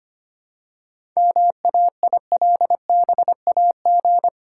Morse code, heard as a keyed tone, about 25 wpm, 690 Hz. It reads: MAILBAG
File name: practice / 25wpm / Easy Words with Hints - 25wpm